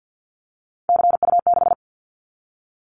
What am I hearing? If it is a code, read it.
Morse code, heard as a keyed tone, about 50 wpm, 700 Hz. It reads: CU6
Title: Callsign Prefixes for Top 100 DXCC - 50wpm